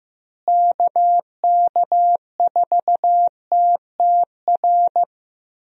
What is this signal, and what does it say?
Morse code, heard as a keyed tone, about 15 wpm, 700 Hz. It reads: KK4TTR
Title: USA Call Signs - 6 Characters - Repeated 2x - 15wpm